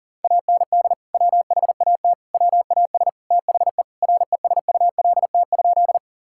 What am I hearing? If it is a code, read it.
Morse code, heard as a keyed tone, about 40 wpm, 700 Hz. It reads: AND WHAT WAS THE RESULT?
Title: Sentences from Top 1000 Words - Review Part 5 - 40wpm